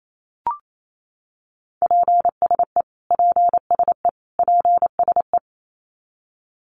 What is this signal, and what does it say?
Morse code, heard as a keyed tone, about 28 wpm, 700 Hz. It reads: PSE PSE PSE
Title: Abbreviations Repeated 3x - 28wpm